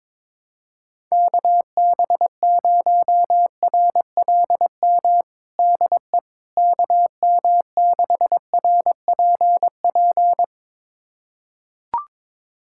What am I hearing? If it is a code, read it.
Morse code, heard as a keyed tone, about 22 wpm, 700 Hz. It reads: KB0RLM DE KM6RPP
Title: USA Callsign DE Callsign - 6 Characters - 22wpm